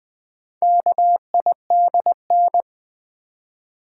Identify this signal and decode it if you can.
Morse code, heard as a keyed tone, about 20 wpm, 700 Hz. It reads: KIDN
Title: Build A Word - 2 Letters At A Time - 6-Letter Words - 20wpm